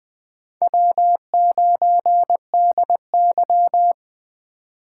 Morse code, heard as a keyed tone, about 20 wpm, 700 Hz. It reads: W9DY
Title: K1USN SST QSO Practice - 20wpm